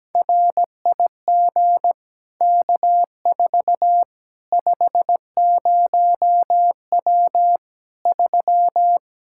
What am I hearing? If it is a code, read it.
Morse code, heard as a keyed tone, about 17 wpm, 700 Hz. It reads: RIG K4 50W 3
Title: Practice QSO - Exchange 2 - Info - 17wpm